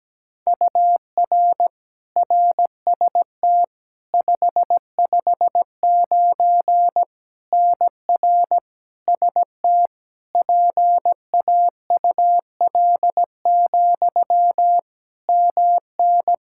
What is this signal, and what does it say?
Morse code, heard as a keyed tone, about 17 wpm, 700 Hz. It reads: UR RST 559 NR ST PAUL, MN
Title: Practice QSO - Exchange 1 - Beginning - 17wpm